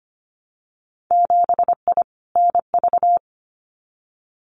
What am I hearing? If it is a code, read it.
Morse code, heard as a keyed tone, about 25 wpm, 700 Hz. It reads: 7I NV
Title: Winter Field Day Exchanges Encore 25wpm